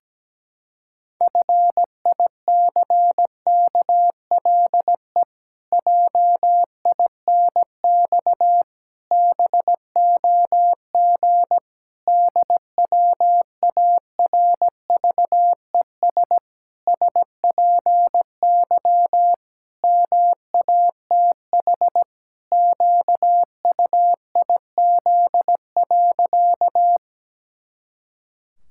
Morse code, heard as a keyed tone, about 17 wpm, 700 Hz. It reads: FICKLE JINX BOG DWARVES SPY MATH QUIZ.
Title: Pangram Challenge v2 - 17wpm